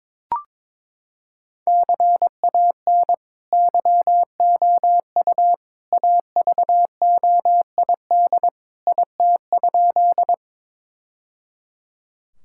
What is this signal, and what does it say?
Morse code, heard as a keyed tone, about 22 wpm, 700 Hz. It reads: CAN YOU AVOID IT?
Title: Sentences from Top 1000 Words - Review Part 6 - 22wpm